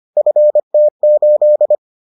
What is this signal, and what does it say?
Morse code, heard as a keyed tone, about 25 wpm, 600 Hz. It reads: FT8